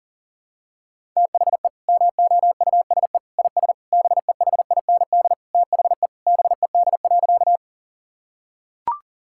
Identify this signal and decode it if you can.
Morse code, heard as a keyed tone, about 40 wpm, 700 Hz. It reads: THE MOUSE IS BEHIND THE BED.
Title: Sentences from Top 1000 Words - Review Part 5 - 40wpm